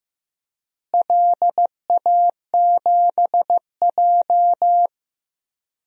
Morse code, heard as a keyed tone, about 16 wpm, 700 Hz. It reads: LA7J
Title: DXCC-100 Call Signs - 4 Characters - Repeated 3x - 15wpm